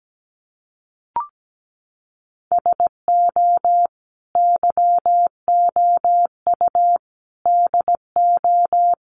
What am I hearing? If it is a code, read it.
Morse code, heard as a keyed tone, about 17 wpm, 700 Hz. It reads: SO YOU DO
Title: Sets of 5 Words Encore 17wpm